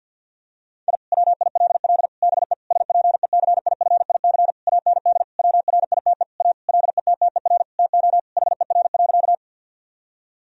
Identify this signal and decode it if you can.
Morse code, heard as a keyed tone, about 50 wpm, 700 Hz. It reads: I WILL BE SPECIFIC AND WRITE A LETTER TO HER.
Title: Sentences from Top 1000 Words - Review Part 3 - 50wpm